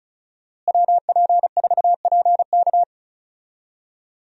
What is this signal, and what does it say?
Morse code, heard as a keyed tone, about 36 wpm, 700 Hz. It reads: WP4PK
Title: USA Call Signs - 5 Characters - Repeated 2x - 35wpm